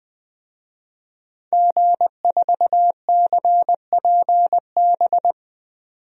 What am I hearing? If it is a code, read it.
Morse code, heard as a keyed tone, about 20 wpm, 700 Hz. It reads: G4CPB
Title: DXCC-100 Call Signs - 5 Characters - 20wpm